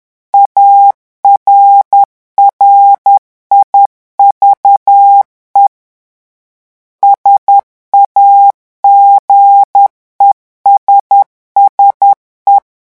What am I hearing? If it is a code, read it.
Morse code, heard as a keyed tone, about 10 wpm, 800 Hz. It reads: ARRIVE SAGESSE
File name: morse mp3 / revision francais 10 mots minute